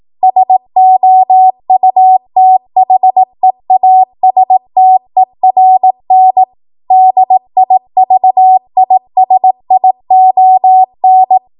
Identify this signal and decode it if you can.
Morse code, heard as a keyed tone, about 18 wpm, 750 Hz. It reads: SOUTHEASTERN DIVISION